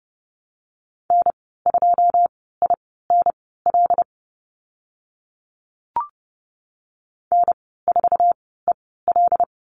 Morse code, heard as a keyed tone, about 30 wpm, spaced as for 17 wpm, 700 Hz. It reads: N2INL N4EL
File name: Lesson 14g - Number 2 - Call Signs 30wpm